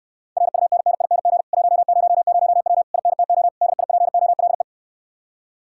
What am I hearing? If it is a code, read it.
Morse code, heard as a keyed tone, about 69 wpm, 700 Hz. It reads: RUNNING 100W INTO DIPOLE